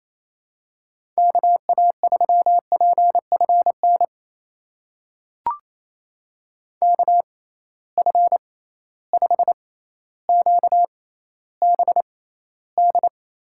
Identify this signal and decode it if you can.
Morse code, heard as a keyed tone, about 28 wpm, spaced as for 14 wpm, 700 Hz. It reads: KA3PFN KF5QBD
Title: USA Call Signs - 6 Characters - 6x Character Spacing - 28wpm